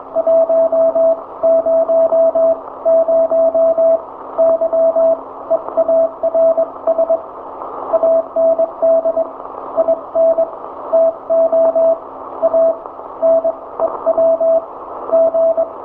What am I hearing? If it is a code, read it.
Morse code, heard as a keyed tone, about 22 wpm, spaced as for 18 wpm, 660 Hz, with radio noise. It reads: 100 YEARS AND INTO A NEW G
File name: hamradio / last 500khz cw